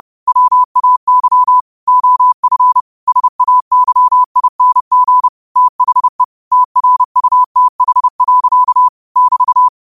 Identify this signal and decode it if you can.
Morse code, heard as a keyed tone, about 30 wpm, 1000 Hz. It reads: WAY OF SAYING THE TRUTH. <BT>
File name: qotdfiles / 30wpm